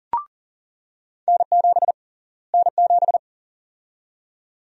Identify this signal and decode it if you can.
Morse code, heard as a keyed tone, about 38 wpm, 700 Hz. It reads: N7 N7
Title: USA Call Sign Prefixes - Repeated 2x - 40wpm